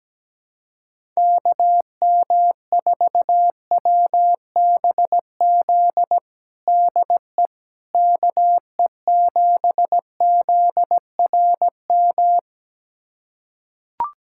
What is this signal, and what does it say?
Morse code, heard as a keyed tone, about 17 wpm, 700 Hz. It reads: KM4WBZ DE KE7ZRM
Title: USA Callsign DE Callsign 17wpm